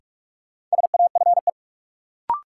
Morse code, heard as a keyed tone, about 41 wpm, 700 Hz. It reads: SAFE
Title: Sentences from Top 900 Words - New Word Drill - 45wpm